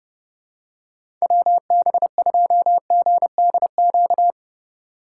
Morse code, heard as a keyed tone, about 30 wpm, 700 Hz. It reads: WB2GDQ
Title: USA Call Signs - 6 Characters - 8x Character Spacing - 30wpm